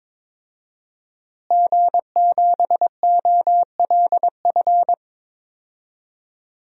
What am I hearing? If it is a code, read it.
Morse code, heard as a keyed tone, about 21 wpm, 700 Hz. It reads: G7OLF
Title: DXCC-100 Call Signs - 5 Characters - Repeated 2x - 22wpm